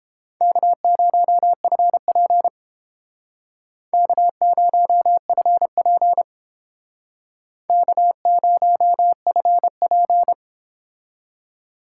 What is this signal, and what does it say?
Morse code, heard as a keyed tone, about 32 wpm, 700 Hz. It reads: K0FP K0FP K0FP